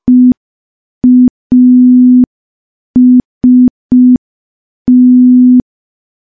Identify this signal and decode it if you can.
Morse code, heard as a keyed tone, about 5 wpm, 250 Hz. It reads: EAST
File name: morse / morse-training-05WPM